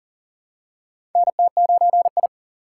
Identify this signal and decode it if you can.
Morse code, heard as a keyed tone, about 40 wpm, 700 Hz. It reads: NT9I
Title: USA Call Signs - 2x Character Spacing - 40wpm